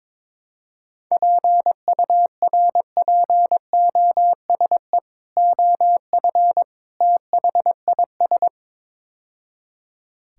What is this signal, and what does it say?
Morse code, heard as a keyed tone, about 22 wpm, 700 Hz. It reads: PURPOSE OF THIS
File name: practice / 22wpm / Build A Phrase - 5-Word Phrases - 22wpm